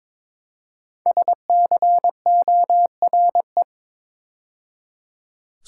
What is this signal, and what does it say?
Morse code, heard as a keyed tone, about 22 wpm, 700 Hz. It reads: SCORE
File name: practice / 22wpm / 5 Letter Words - First 3 Letters Given - 22wpm